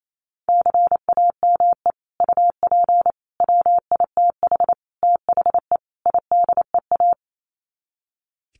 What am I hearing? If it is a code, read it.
Morse code, heard as a keyed tone, about 28 wpm, 700 Hz. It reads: CAME UP WITH THE IDEA
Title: Sets of 5 Words Encore2 28wpm